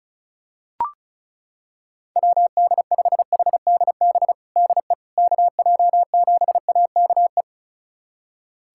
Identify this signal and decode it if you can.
Morse code, heard as a keyed tone, about 36 wpm, 700 Hz. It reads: WD5HDB DE KJ7AKE
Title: USA Callsign DE Callsign - 6 Characters - 35wpm